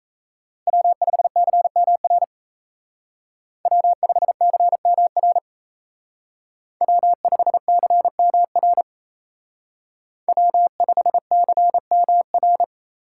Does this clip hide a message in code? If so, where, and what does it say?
Yes: Morse code from 0.7 to 12.6 s, heard as a keyed tone, about 36 wpm, 700 Hz. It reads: W5CMR W5CMR W5CMR W5CMR